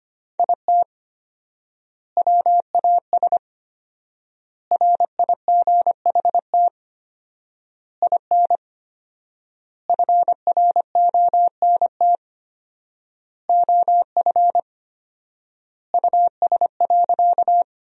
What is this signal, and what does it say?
Morse code, heard as a keyed tone, about 25 wpm, 700 Hz. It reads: IT WAS RIGHT IN FRONT OF US.
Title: 4x Word-Spacing - Sentences from Top 500 Words - 25wpm